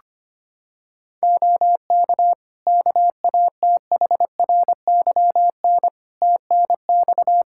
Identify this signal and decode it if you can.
Morse code, heard as a keyed tone, about 25 wpm, 700 Hz. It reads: OK KATHRYN TNX
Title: K1USN SST QSO Practice 25wpm